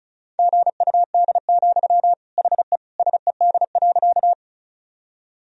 Morse code, heard as a keyed tone, about 34 wpm, 690 Hz. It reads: GUD, HE SED.
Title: Sentences from Top 200 Words and Abbreviations 35wpm